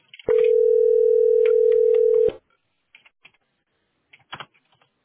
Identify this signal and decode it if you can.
Ringback tone (North America), 440 + 480 Hz.